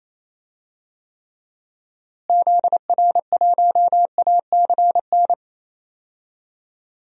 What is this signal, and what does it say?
Morse code, heard as a keyed tone, about 28 wpm, 700 Hz. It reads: ZR1ACN
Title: DXCC-100 Call Signs - 6 Characters - Repeated 3x - 28wpm